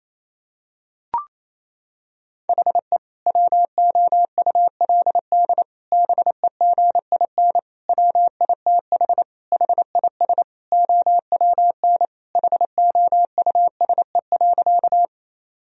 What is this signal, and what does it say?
Morse code, heard as a keyed tone, about 28 wpm, 700 Hz. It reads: HE WOULD BEGIN WITH HIS OWN HOUSE.
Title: Sentences from Top 1000 Words - Review Part 1 - 28wpm